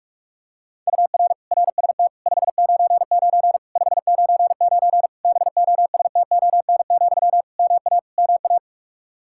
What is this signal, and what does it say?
Morse code, heard as a keyed tone, about 46 wpm, 700 Hz. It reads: UR RST 599 599 BOSTON, MA MA
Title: Practice QSO - Exchange 1 - Beginning - 45wpm